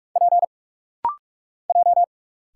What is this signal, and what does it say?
Morse code, heard as a keyed tone, about 46 wpm, 700 Hz. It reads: P J
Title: Single Letters - Rapid-Fire 45wpm